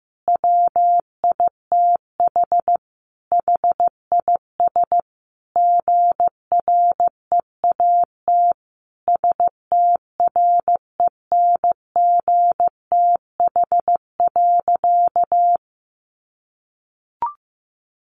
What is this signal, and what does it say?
Morse code, heard as a keyed tone, about 15 wpm, 700 Hz. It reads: WITH HIS GREAT STRENGTH.
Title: Sentences from Top 1300 Words 15wpm